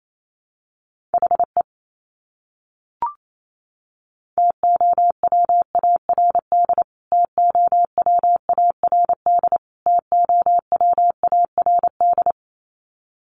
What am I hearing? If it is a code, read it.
Morse code, heard as a keyed tone, about 28 wpm, 700 Hz. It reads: HE TOWARD TOWARD TOWARD
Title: Top 500 Words Repeated 28wpm